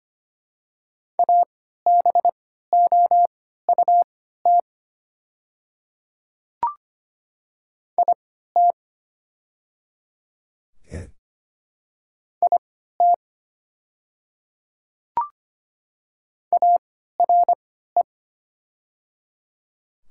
Morse code, heard as a keyed tone, about 25 wpm, spaced as for 14 wpm, 700 Hz. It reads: ABOUT IT IT ARE
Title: 3x Character Spacing - Top 200 Words - 25wpm